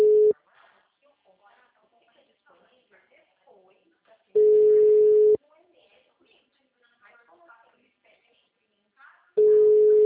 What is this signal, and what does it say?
Ringback tone (Europe, 425 Hz), with speech.
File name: out- -2014- - - .73